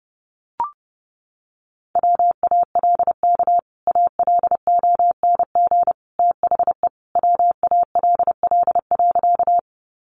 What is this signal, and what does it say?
Morse code, heard as a keyed tone, about 30 wpm, 700 Hz. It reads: WALK ALONG THE WALL.
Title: Sentences from Top 1100 Words 30wpm